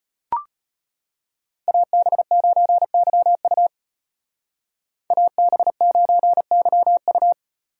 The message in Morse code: AB9YU AB9YU

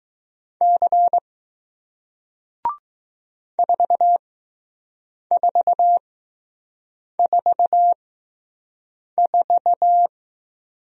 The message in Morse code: C 4444

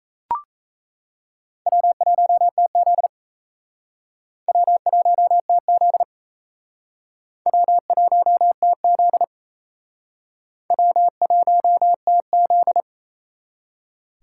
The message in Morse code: W1TZ W1TZ W1TZ W1TZ

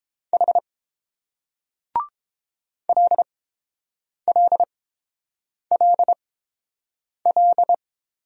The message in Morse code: H LLLL

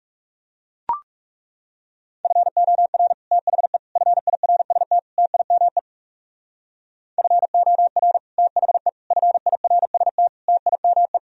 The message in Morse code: FOR THE FIRST TIME FOR THE FIRST TIME

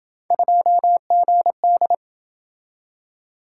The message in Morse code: 2GD

